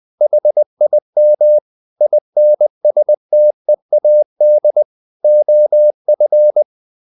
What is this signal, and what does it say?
Morse code, heard as a keyed tone, about 20 wpm, 600 Hz. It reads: HIM INSTEAD OF
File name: the old man and the sea / the old man and the sea part 1 20wpm